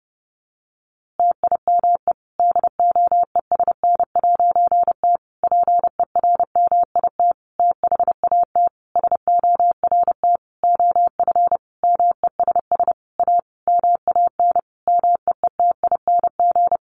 Morse code, heard as a keyed tone, about 30 wpm, 700 Hz. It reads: TIME DOESN'T PERMIT THAT SORT OF MESS A MAN MEETING
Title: A Sound Of Thunder 30wpm